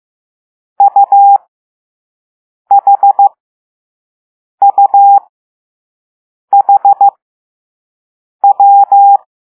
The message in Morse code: UHUHW